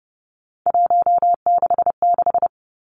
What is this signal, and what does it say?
Morse code, heard as a keyed tone, about 32 wpm, 700 Hz. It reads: 166
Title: Three Challenging Letter-Number Combinations 30wpm